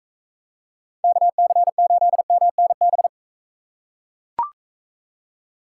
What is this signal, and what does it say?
Morse code, heard as a keyed tone, about 42 wpm, 700 Hz. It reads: KC8MNB